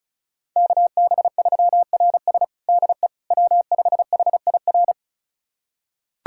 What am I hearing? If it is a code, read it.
Morse code, heard as a keyed tone, about 34 wpm, 700 Hz. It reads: KB3RS DE W5HIR
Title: USA Callsign DE Callsign 35wpm